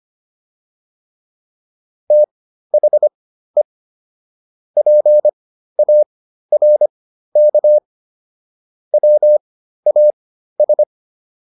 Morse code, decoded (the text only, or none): THE PARK WAS